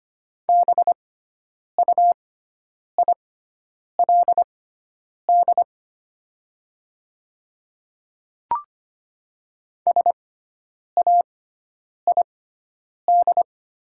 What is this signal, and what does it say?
Morse code, heard as a keyed tone, about 25 wpm, spaced as for 9 wpm, 700 Hz. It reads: BUILD SAID
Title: 6x Character Spacing - Top 200 Words - 25wpm